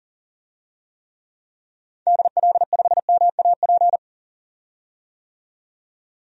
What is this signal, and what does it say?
Morse code, heard as a keyed tone, about 37 wpm, 700 Hz. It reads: DL5MAP
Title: DXCC-100 Call Signs - 6 Characters - 40wpm